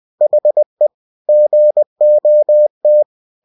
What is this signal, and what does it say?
Morse code, heard as a keyed tone, about 20 wpm, 600 Hz. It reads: HE GOT
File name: cw-books / winnie-the-pooh / winnie-20wpm-page-0083